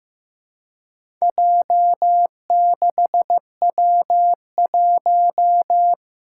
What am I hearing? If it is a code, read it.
Morse code, heard as a keyed tone, about 15 wpm, 700 Hz. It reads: J6W1
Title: Four Challenging Letter-Number Combinations 15wpm